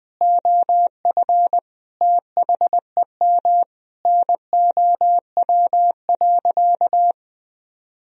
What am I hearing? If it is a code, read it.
Morse code, heard as a keyed tone, about 20 wpm, 700 Hz. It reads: OF THEM NOW.
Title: Sentences from Top 1000 Words - Review Part 1 - 20wpm